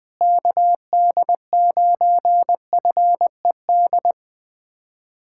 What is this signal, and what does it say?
Morse code, heard as a keyed tone, about 21 wpm, 700 Hz. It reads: KD9FED
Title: USA Callsign DE Callsign 20wpm